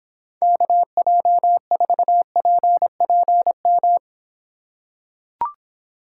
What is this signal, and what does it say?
Morse code, heard as a keyed tone, about 26 wpm, 700 Hz. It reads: KJ4PPM